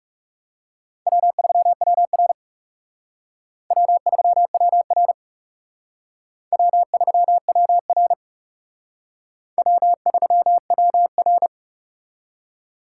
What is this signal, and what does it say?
Morse code, heard as a keyed tone, about 39 wpm, 700 Hz. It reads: W3WR W3WR W3WR W3WR